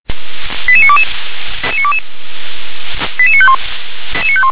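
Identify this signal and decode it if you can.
Five-tone selective call (ZVEI), heard as quick runs of five tones, tones of about 70 ms. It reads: 88922 88951
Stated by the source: Five-tone ZVEI selective call received on an SDR.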